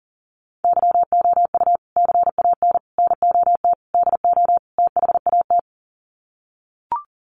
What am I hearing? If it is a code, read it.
Morse code, heard as a keyed tone, about 38 wpm, 700 Hz. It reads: YOU CAN NOT DO THAT
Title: Sets of 5 Words 40wpm